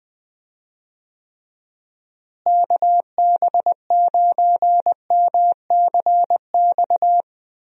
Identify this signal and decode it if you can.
Morse code, heard as a keyed tone, about 20 wpm, 700 Hz. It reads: KB9MCX
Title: USA Call Signs - 6 Characters - Repeated 3x - 20wpm